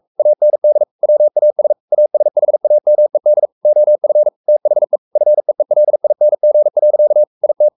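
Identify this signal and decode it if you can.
Morse code, heard as a keyed tone, about 44 wpm, 600 Hz. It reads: AND WAS ASHAMED OF THE FEELING. IT
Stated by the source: Morse code at 43 wpm sending Ken Follett's novel Never in English, from CW med Gnister.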